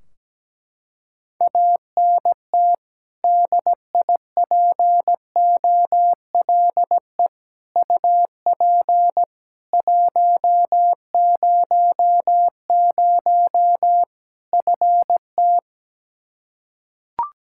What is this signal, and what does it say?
Morse code, heard as a keyed tone, about 17 wpm, 700 Hz. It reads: ANT DIPOLE UP 100 FT
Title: Practice QSO - Exchange 2 - Info - 17wpm